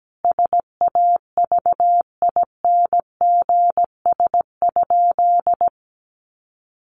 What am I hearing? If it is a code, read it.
Morse code, heard as a keyed tone, about 17 wpm, 700 Hz. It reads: SAVINGS?
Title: Sentences from Top 1200 Words 17wpm